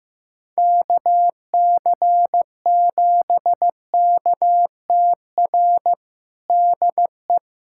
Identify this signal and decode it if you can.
Morse code, heard as a keyed tone, about 15 wpm, 700 Hz. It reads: KC7KTR DE
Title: USA Callsign DE Callsign 15wpm